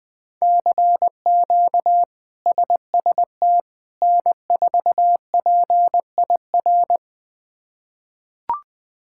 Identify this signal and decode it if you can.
Morse code, heard as a keyed tone, about 21 wpm, 700 Hz. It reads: CQ SST N4PIR
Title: K1USN SST QSO Practice - 20wpm